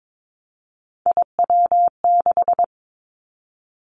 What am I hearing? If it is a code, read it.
Morse code, heard as a keyed tone, about 22 wpm, 700 Hz. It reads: IW6
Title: Three Challenging Letter-Number Combinations 22wpm